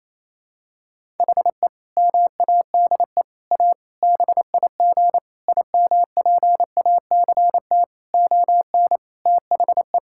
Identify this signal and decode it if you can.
Morse code, heard as a keyed tone, about 28 wpm, 700 Hz. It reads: HE MADE A BIG IMPACT ON THE
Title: Sentences from Top 1000 Words - Review Part 6 - 28wpm